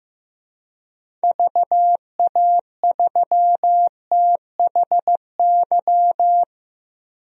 Morse code, heard as a keyed tone, about 15 wpm, 700 Hz. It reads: VA3THY
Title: DXCC-100 Call Signs - 6 Characters - Repeated 3x - 15wpm